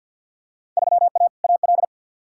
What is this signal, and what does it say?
Morse code, heard as a keyed tone, about 47 wpm, 700 Hz. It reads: 3A AL
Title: Field Day Exchanges - 50wpm